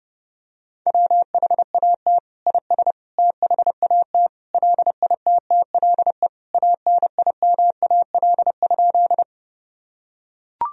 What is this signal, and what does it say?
Morse code, heard as a keyed tone, about 32 wpm, 700 Hz. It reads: WHAT IS THAT LITTLE ANIMAL?
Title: Sentences from Top 1000 Words - Review Part 1 - 30wpm